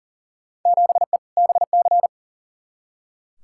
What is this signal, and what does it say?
Morse code, heard as a keyed tone, about 40 wpm, 700 Hz. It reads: 7E BC
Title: Field Day Exchanges Encore - 40wpm